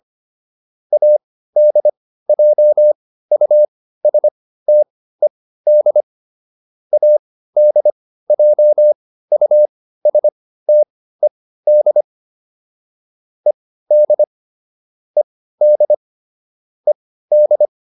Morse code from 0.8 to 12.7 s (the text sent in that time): ADJUSTED ADJUSTED